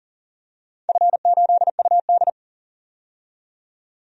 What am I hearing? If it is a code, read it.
Morse code, heard as a keyed tone, about 40 wpm, 700 Hz. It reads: F8UD
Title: DXCC-100 Call Signs - 4 Characters - 40wpm